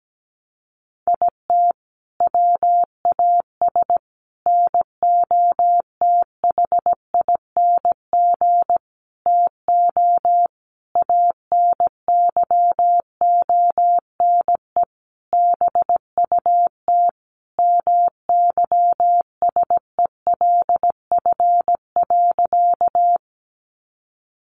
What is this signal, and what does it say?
Morse code, heard as a keyed tone, about 17 wpm, 700 Hz. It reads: IT WAS NOTHING TO ANYONE BUT MYSELF.
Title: Sentences from Top 1300 Words 17wpm